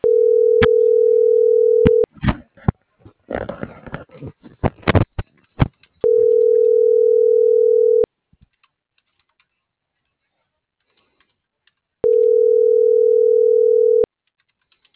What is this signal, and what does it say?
Ringback tone (North America), 440 + 480 Hz.